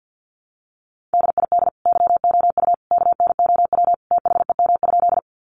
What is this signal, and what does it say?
Morse code, heard as a keyed tone, about 50 wpm, 700 Hz. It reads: DID YOU KNOW THEM?
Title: Sentences from Top 100 Words 50wpm